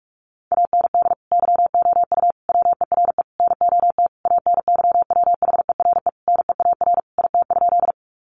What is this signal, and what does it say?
Morse code, heard as a keyed tone, about 44 wpm, 700 Hz. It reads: AND YOU WERE NOT ANYWHERE NEAR IT?
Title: Sentences from Top 1400 Words 45wpm